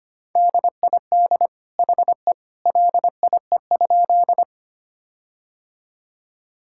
Morse code, heard as a keyed tone, about 25 wpm, 700 Hz. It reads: DID HE LIE?
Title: Sentences from Top 1600 Words 25wpm